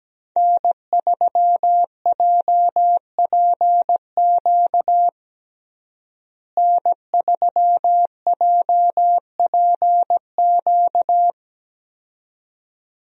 Morse code, heard as a keyed tone, about 17 wpm, 700 Hz. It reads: N3JPQ N3JPQ
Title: USA Call Signs - 5 Characters - Repeated 2x - 17wpm